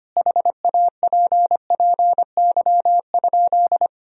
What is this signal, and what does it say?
Morse code, heard as a keyed tone, about 25 wpm, 700 Hz. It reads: HAPPY?
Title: Sentences from Top 1000 Words - Review Part 2 - 25wpm